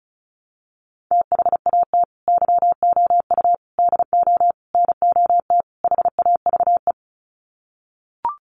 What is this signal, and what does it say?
Morse code, heard as a keyed tone, about 34 wpm, 700 Hz. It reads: THAT YOU DO NOT HAVE
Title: Sets of 5 Words Encore 35wpm